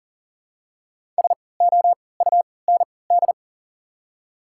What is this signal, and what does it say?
Morse code, heard as a keyed tone, about 38 wpm, spaced as for 22 wpm, 700 Hz. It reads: SOUND
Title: 3x Character Spacing - Top 300 Words - 40wpm